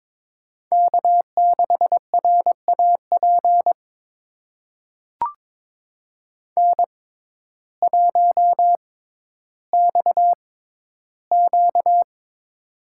Morse code, heard as a keyed tone, about 22 wpm, spaced as for 12 wpm, 700 Hz. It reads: K6RAP N1XQ
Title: USA Call Signs - 6x Character Spacing - 22wpm